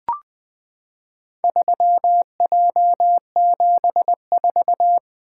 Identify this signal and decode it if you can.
Morse code, heard as a keyed tone, about 20 wpm, 700 Hz. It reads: 3J74
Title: Four Letter-Number Combinations - 20wpm